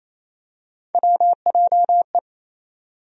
Morse code, heard as a keyed tone, about 28 wpm, 700 Hz. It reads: WJE
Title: USA Call Sign Postfixes - 3 Letters 28wpm